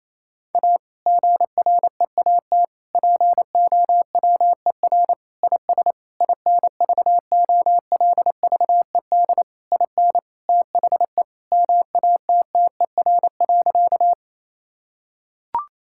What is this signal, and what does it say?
Morse code, heard as a keyed tone, about 28 wpm, 700 Hz. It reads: A GREAT POWER IS INVOLVED IN THE MATTER.
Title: Sentences from Top 1000 Words - Review Part 5 - 28wpm